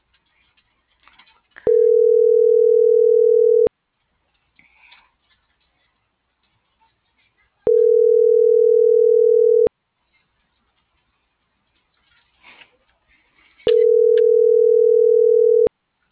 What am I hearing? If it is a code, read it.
Ringback tone (North America), 440 + 480 Hz.